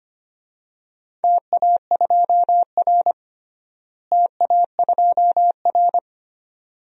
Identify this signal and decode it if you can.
Morse code, heard as a keyed tone, about 25 wpm, 700 Hz. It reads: TA2R TA2R
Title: DXCC-100 Call Signs - 4 Characters - Repeated 3x - 25wpm